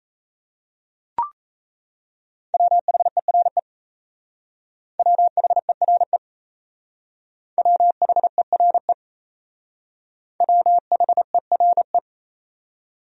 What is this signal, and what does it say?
Morse code, heard as a keyed tone, about 35 wpm, 700 Hz. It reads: WHERE WHERE WHERE WHERE
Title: Speed-Racing Top 200 Words 28wpm